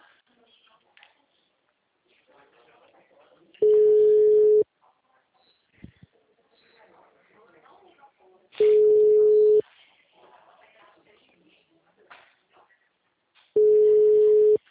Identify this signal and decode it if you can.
Ringback tone (Europe, 425 Hz), with speech.
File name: out- -2005- - - .30513